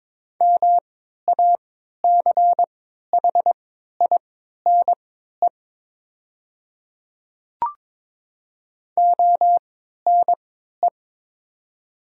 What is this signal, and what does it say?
Morse code, heard as a keyed tone, about 22 wpm, spaced as for 12 wpm, 700 Hz. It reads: MACHINE ONE
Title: 3x Character Spacing - Top 500 Words - 22wpm